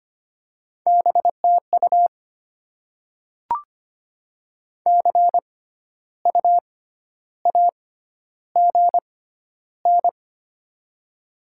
Morse code, heard as a keyed tone, about 25 wpm, spaced as for 12 wpm, 700 Hz. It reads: BTU CU AGN
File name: Top 20 QSO Elements - 6x Character Spacing - 25wpm